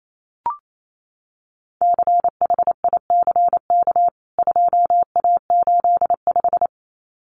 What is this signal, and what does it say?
Morse code, heard as a keyed tone, about 28 wpm, 700 Hz. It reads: CHICK 2A85
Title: CWT Exchange 28wpm